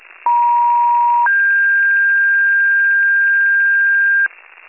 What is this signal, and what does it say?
Two-tone page, heard as a one-second tone, then a three-second tone, about 950 then 1600 Hz, with noise. It is heard from 0.3 to 4.3 s.
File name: pager / 83 RESCUE DUTY / wcalert